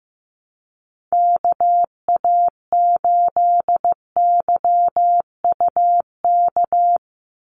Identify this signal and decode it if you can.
Morse code, heard as a keyed tone, about 15 wpm, 700 Hz. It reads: KA8YUK
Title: USA Call Signs 15wpm